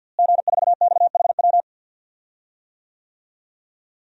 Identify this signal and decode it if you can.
Morse code, heard as a keyed tone, about 46 wpm, 700 Hz. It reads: G4XHW
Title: DXCC-100 Call Signs - 5 Characters - Repeated 2x - 50wpm